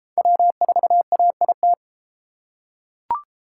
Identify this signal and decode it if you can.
Morse code, heard as a keyed tone, about 30 wpm, 700 Hz. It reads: W4AIT